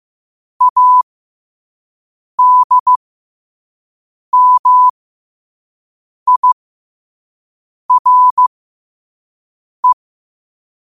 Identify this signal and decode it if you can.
Morse code, heard as a keyed tone, about 15 wpm, spaced as for 5 wpm, 1000 Hz. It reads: ADMIRE
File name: qotdfiles / 5wpm